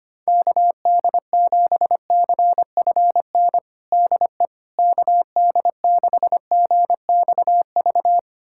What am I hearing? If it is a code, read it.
Morse code, heard as a keyed tone, about 25 wpm, 700 Hz. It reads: KD7CFN DE KD6GXV
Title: USA Callsign DE Callsign 25wpm